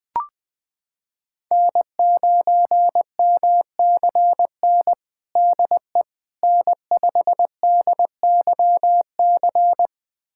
Morse code, heard as a keyed tone, about 20 wpm, 700 Hz. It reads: N9MCN DE N5DYC
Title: USA Callsign DE Callsign - 5 Characters - 20wpm